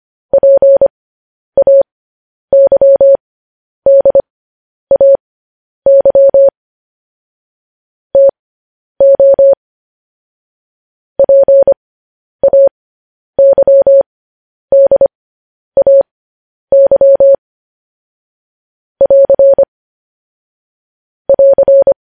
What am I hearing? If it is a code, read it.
Morse code, heard as a keyed tone, about 25 wpm, spaced as for 10 wpm, 550 Hz. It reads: PAYDAY TO PAYDAY <AR> <AR>
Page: news-headlines-in-morse-code-at-10-wpm